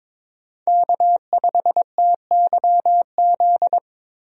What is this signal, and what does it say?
Morse code, heard as a keyed tone, about 22 wpm, 700 Hz. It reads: K5TYZ
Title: USA Call Signs - 5 Characters - Repeated 3x - 22wpm